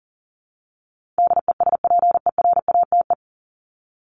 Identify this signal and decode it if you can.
Morse code, heard as a keyed tone, about 40 wpm, 700 Hz. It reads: DESPERATE
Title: Top 1751-2000 Words 40wpm